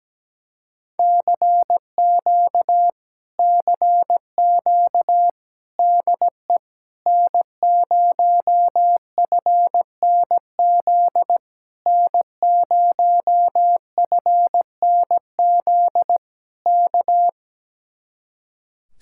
Morse code, heard as a keyed tone, about 17 wpm, 700 Hz. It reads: CQ CQ DE N0FNZ N0FNZ K
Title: Speed-Racing Common QSO Phrases Encore - 17wpm